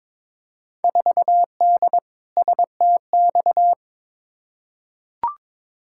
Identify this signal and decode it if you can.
Morse code, heard as a keyed tone, about 22 wpm, 700 Hz. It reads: 4D STX
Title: Field Day Exchanges Encore - 22wpm